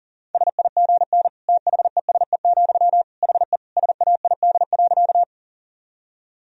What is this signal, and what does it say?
Morse code, heard as a keyed tone, about 40 wpm, 700 Hz. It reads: SIGN THESE, HE SAID.
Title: Sentences from Top 1000 Words - Review Part 1 - 40wpm